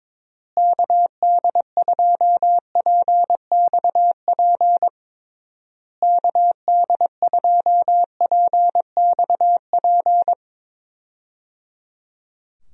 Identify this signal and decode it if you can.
Morse code, heard as a keyed tone, about 22 wpm, 700 Hz. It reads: KD2PXP KD2PXP
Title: USA Call Signs - 6 Characters - Repeated 3x - 22wpm